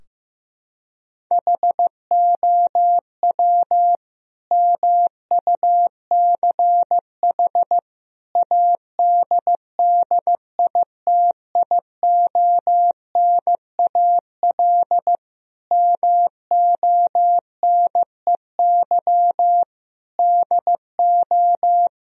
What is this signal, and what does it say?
Morse code, heard as a keyed tone, about 15 wpm, 700 Hz. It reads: HOW MUCH ADDITIONAL MONEY DO
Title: Sentences from Top 1000 Words - Review Part 4 - 15wpm